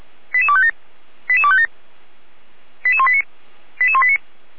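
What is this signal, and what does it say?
Five-tone selective call (ZVEI), heard as quick runs of five tones, tones of about 70 ms. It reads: 88257 88257 88189 88189
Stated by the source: ZVEI five-tone selective call 88505 received by SDR.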